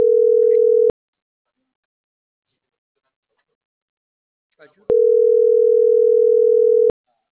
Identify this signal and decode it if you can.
Ringback tone (North America), 440 + 480 Hz.